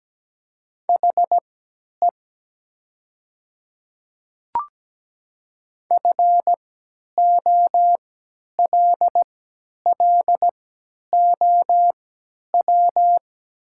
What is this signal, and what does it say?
Morse code, heard as a keyed tone, about 17 wpm, spaced as for 10 wpm, 700 Hz. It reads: HE FOLLOW
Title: 3x Character Spacing - Top 300 Words - 17wpm